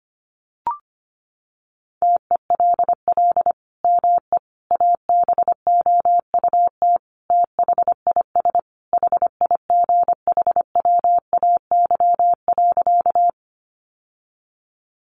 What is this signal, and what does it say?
Morse code, heard as a keyed tone, about 25 wpm, 700 Hz. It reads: TELL ME ABOUT THIS HIGHWAY.
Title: Sentences from Top 1400 Words 25wpm